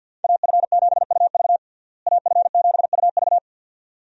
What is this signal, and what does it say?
Morse code, heard as a keyed tone, about 46 wpm, 700 Hz. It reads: AF7UV AF7UV
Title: USA Call Signs - 5 Characters - Repeated 2x - 50wpm